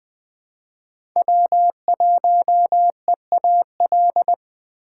Morse code, heard as a keyed tone, about 23 wpm, 700 Hz. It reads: W1EAL
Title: Speed Racing - USA Call Signs 20wpm